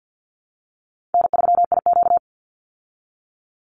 Morse code, heard as a keyed tone, about 50 wpm, 700 Hz. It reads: N3IQ
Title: Easy Call Signs Repeated 2x - 50wpm